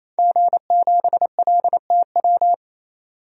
Morse code, heard as a keyed tone, about 28 wpm, 700 Hz. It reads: G7LTW
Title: DXCC-100 Call Signs - 5 Characters - 28wpm